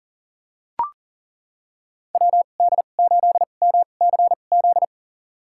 Morse code, heard as a keyed tone, about 40 wpm, spaced as for 29 wpm, 700 Hz. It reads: WD8MCZ
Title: USA Call Signs - 6 Characters - 2x Character Spacing - 40wpm